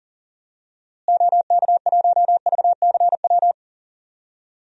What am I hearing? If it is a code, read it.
Morse code, heard as a keyed tone, about 37 wpm, 700 Hz. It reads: OK1VCW
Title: DXCC-100 Call Signs - 6 Characters - Repeated 3x - 40wpm